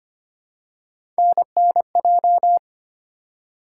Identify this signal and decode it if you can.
Morse code, heard as a keyed tone, about 25 wpm, 700 Hz. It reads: NNJ
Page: morse-code-ninja-practice-25wpm